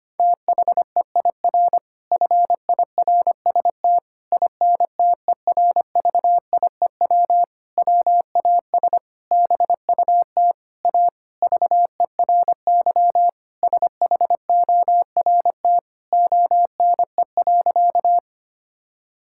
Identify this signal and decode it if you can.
Morse code, heard as a keyed tone, about 25 wpm, 700 Hz. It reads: THEIR FIRST INTERVIEW WAS BUT A VERY SHORT ONE.
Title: Sentences from Top 1600 Words 25wpm